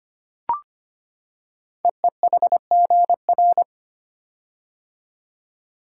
Morse code, heard as a keyed tone, about 25 wpm, 700 Hz. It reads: EEHGR
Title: Code Groups - 5 Letters - 25wpm